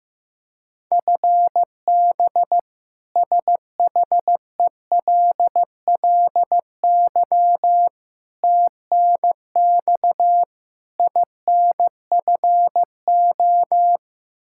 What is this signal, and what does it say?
Morse code, heard as a keyed tone, about 15 wpm, 700 Hz. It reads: FB SHELLY TNX INFO
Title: K1USN SST QSO Practice 15wpm